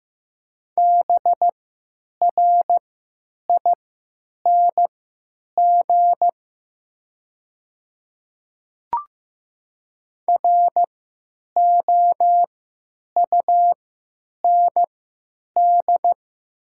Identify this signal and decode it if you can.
Morse code, heard as a keyed tone, about 15 wpm, spaced as for 9 wpm, 700 Hz. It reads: BRING ROUND
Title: 3x Character Spacing - Top 500 Words - 15wpm